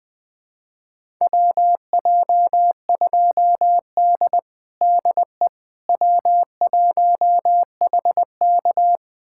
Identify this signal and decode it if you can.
Morse code, heard as a keyed tone, about 21 wpm, 700 Hz. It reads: WJ2D DE W1HK
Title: USA Callsign DE Callsign - 4 Characters - 20wpm